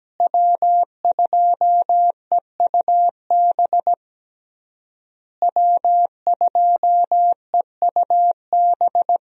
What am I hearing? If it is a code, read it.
Morse code, heard as a keyed tone, about 17 wpm, 700 Hz. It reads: W2EUB W2EUB
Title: USA Call Signs - 5 Characters - Repeated 3x - 17wpm